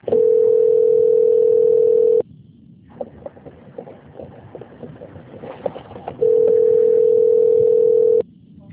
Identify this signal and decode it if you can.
Ringback tone (North America), 440 + 480 Hz.